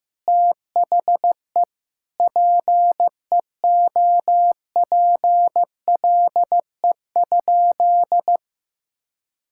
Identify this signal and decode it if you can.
Morse code, heard as a keyed tone, about 15 wpm, 700 Hz. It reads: THE PEOPLE?
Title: Sentences from Top 1000 Words - Review Part 5 - 15wpm